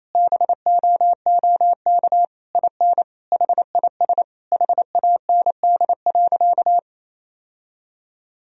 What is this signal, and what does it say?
Morse code, heard as a keyed tone, about 28 wpm, 700 Hz. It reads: BOOK IN HIS HAND.
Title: Sentences from Top 1000 Words - Review Part 1 - 28wpm